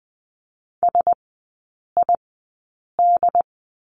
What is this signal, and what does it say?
Morse code, heard as a keyed tone, about 20 wpm, spaced as for 8 wpm, 700 Hz. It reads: SID
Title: 074 Lesson 13e - Letter L - Three Characters 20wpm